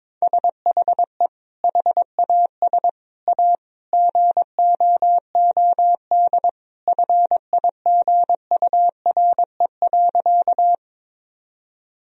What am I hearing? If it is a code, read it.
Morse code, heard as a keyed tone, about 22 wpm, 700 Hz. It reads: SHE HAS A GOOD FIGURE.
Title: Sentences from Top 1000 Words - Review Part 2 - 22wpm